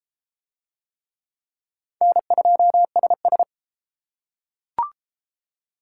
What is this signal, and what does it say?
Morse code, heard as a keyed tone, about 28 wpm, 700 Hz. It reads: N2SS